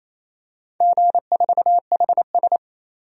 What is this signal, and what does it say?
Morse code, heard as a keyed tone, about 28 wpm, 700 Hz. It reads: G4HS
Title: DXCC-100 Call Signs - 4 Characters - 28wpm